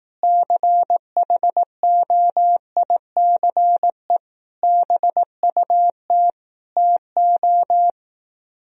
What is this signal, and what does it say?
Morse code, heard as a keyed tone, about 18 wpm, 700 Hz. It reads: CHOICE BUT TO
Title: Speed-Racing - Sets of 5 Words 15wpm